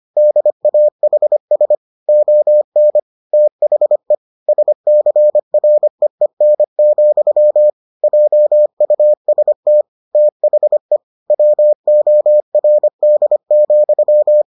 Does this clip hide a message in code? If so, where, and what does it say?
Yes: Morse code from 0.2 to 14.4 s, heard as a keyed tone, about 25 wpm, 600 Hz. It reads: DAHS ON THE SCREEN, JUST THE WORD,